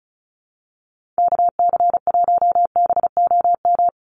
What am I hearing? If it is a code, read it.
Morse code, heard as a keyed tone, about 34 wpm, 700 Hz. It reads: KC1BOM
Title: USA Call Signs 35wpm